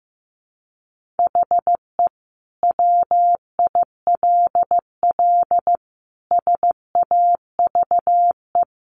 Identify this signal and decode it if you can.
Morse code, heard as a keyed tone, about 15 wpm, 700 Hz. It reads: HE WILL SAVE
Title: Sentences from Top 1100 Words 15wpm